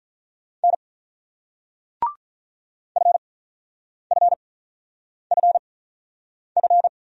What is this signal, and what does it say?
Morse code, heard as a keyed tone, about 47 wpm, spaced as for 9 wpm, 690 Hz. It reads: N FFFF